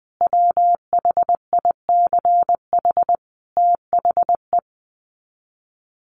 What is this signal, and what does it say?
Morse code, heard as a keyed tone, about 20 wpm, 700 Hz. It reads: WHICH THE
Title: Sets of 5 Words Encore2 20wpm